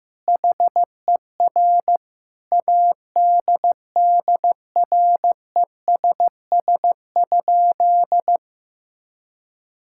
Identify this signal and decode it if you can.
Morse code, heard as a keyed tone, about 16 wpm, 700 Hz. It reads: HER ADDRESS?
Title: Sentences from Top 1000 Words - Review Part 6 - 15wpm